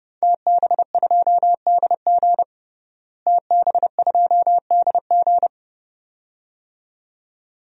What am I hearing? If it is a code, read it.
Morse code, heard as a keyed tone, about 32 wpm, 700 Hz. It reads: TB2DG TB2DG
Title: DXCC-100 Call Signs - 5 Characters - Repeated 2x - 30wpm